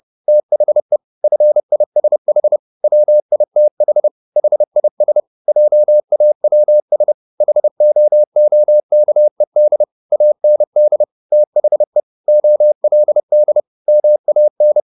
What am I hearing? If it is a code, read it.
Morse code, heard as a keyed tone, about 30 wpm, 600 Hz. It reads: THE FISH WITH HIS JAWS HOOKED AND THE OLD MAN